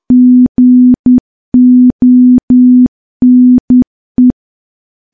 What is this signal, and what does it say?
Morse code, heard as a keyed tone, about 10 wpm, 250 Hz. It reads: GONE
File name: morse / morse-training-10WPM